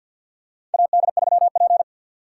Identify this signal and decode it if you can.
Morse code, heard as a keyed tone, about 53 wpm, 700 Hz. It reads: AD3P
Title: USA Call Signs - 4 Characters - 3x Character Spacing - 50wpm